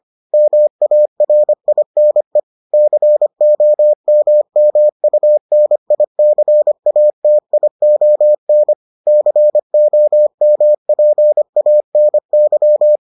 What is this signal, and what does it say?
Morse code, heard as a keyed tone, about 25 wpm, 600 Hz. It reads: MARINE COMMUNICATION COMPANY